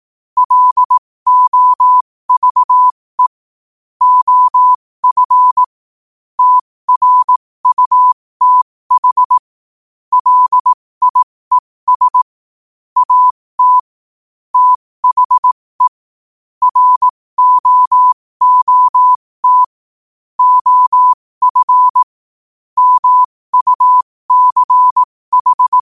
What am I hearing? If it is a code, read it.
Morse code, heard as a keyed tone, about 18 wpm, spaced as for 15 wpm, 1000 Hz. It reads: LOVE OF TRUTH LIES AT THE ROOT OF MUCH
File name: qotdfiles / 15wpm